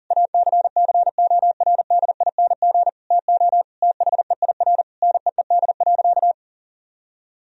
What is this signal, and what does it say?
Morse code, heard as a keyed tone, about 40 wpm, 700 Hz. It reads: ACCORDING TO THEIR NEED.
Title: Sentences from Top 1000 Words - Review Part 4 - 40wpm